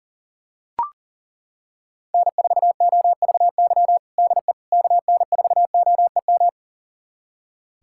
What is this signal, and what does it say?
Morse code, heard as a keyed tone, about 40 wpm, 700 Hz. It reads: N4OVY DE KN4OEM
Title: USA Callsign DE Callsign 40wpm